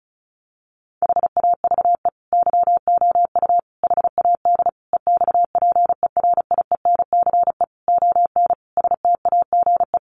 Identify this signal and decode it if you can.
Morse code, heard as a keyed tone, about 34 wpm, 700 Hz. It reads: HAVE YOU HAD EXPERIENCE ON STAGE
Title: Sentences from Top 1200 Words 35wpm